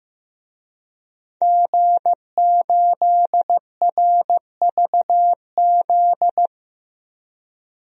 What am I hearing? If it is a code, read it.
Morse code, heard as a keyed tone, about 15 wpm, 700 Hz. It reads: G8RVZ
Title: DXCC-100 Call Signs - 5 Characters - 15wpm